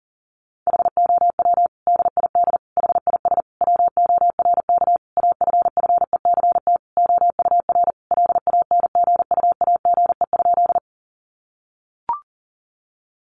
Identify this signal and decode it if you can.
Morse code, heard as a keyed tone, about 40 wpm, 700 Hz. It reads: HOW DID HIS WORK AFFECT OUR LANGUAGE?
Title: Sentences from Top 900 Words 40wpm